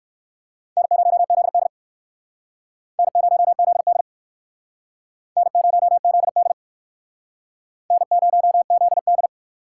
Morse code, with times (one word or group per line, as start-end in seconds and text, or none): 0.8-1.7 N0ZD
3.0-4.0 N0ZD
5.4-6.5 N0ZD
7.9-9.3 N0ZD